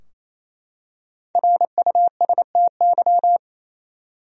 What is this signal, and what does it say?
Morse code, heard as a keyed tone, about 27 wpm, 700 Hz. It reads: RUSTY
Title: USA Common Names - Max 5 Letters - 28wpm